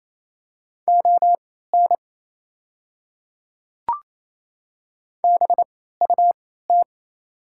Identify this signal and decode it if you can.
Morse code, heard as a keyed tone, about 28 wpm, spaced as for 16 wpm, 700 Hz. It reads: ON BUT
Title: 3x Character Spacing - Top 300 Words - 28wpm